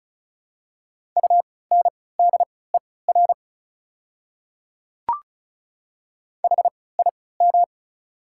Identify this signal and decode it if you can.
Morse code, heard as a keyed tone, about 36 wpm, spaced as for 20 wpm, 700 Hz. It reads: UNDER HIM
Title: 3x Character Spacing - Top 200 Words - 35wpm